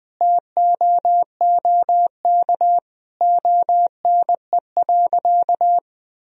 Morse code, heard as a keyed tone, about 20 wpm, 700 Hz. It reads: TOOK ONE.
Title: Sentences from Top 1000 Words - Review Part 1 - 20wpm